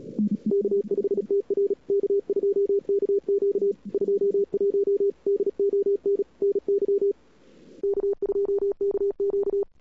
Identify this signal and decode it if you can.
Morse code, heard as a keyed tone, about 39 wpm, 410 Hz. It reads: K5TR K2KQ 21 DON NY K2KQ
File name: cw-sprint-k5tr-op-n5rz